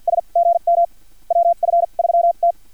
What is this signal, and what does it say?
Morse code, heard as a keyed tone, about 50 wpm, spaced as for 41 wpm, 690 Hz. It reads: S GM WF3T
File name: archived / CRSnet